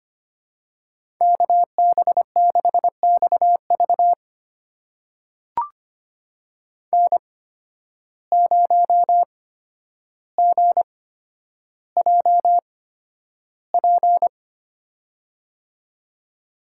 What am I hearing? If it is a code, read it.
Morse code, heard as a keyed tone, about 25 wpm, spaced as for 11 wpm, 700 Hz. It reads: KB6XV N0GJP
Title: USA Call Signs - 5 Characters - 8x Character Spacing - 25wpm